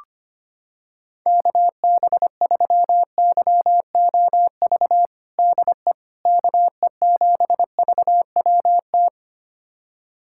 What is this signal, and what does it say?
Morse code, heard as a keyed tone, about 25 wpm, 700 Hz. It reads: KB3YOV DE KE7VWT
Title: USA Callsign DE Callsign - 6 Characters - 25wpm